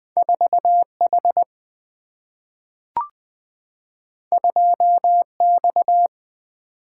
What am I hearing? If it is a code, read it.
Morse code, heard as a keyed tone, about 20 wpm, 700 Hz. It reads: 4H 2X
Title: Two Letter-Number Combinations - 20wpm